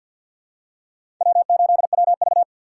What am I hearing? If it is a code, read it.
Morse code, heard as a keyed tone, about 48 wpm, 700 Hz. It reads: W8WV
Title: USA Call Signs - 4 Characters - 3x Character Spacing - 50wpm